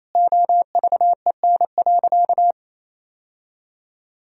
Morse code, heard as a keyed tone, about 28 wpm, 700 Hz. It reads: OVEN.
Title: Sentences from Top 1000 Words - Review Part 4 - 28wpm